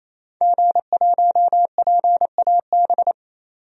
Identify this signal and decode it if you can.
Morse code, heard as a keyed tone, about 27 wpm, 700 Hz. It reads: G1PAB
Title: DXCC-100 Call Signs - 5 Characters - 28wpm